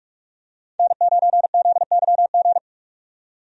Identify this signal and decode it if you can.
Morse code, heard as a keyed tone, about 46 wpm, 700 Hz. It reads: N9ZYG
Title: USA Call Signs - 5 Characters - 45wpm